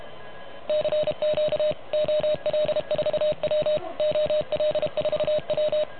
Morse code, heard as a keyed tone, about 31 wpm, 610 Hz. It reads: CQ OL4W OL4W
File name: truSDX / MWC29.8